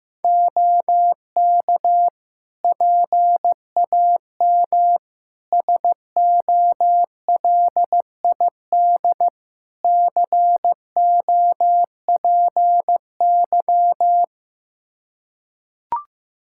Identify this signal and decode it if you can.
Morse code, heard as a keyed tone, about 15 wpm, 700 Hz. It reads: OK PAM SOLID COPY
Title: Practice QSO - Exchange 3 - More Info - 15wpm